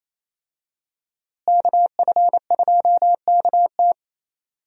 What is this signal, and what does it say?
Morse code, heard as a keyed tone, about 28 wpm, 700 Hz. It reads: KF2KT
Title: USA Call Signs - 5 Characters - Repeated 2x - 28wpm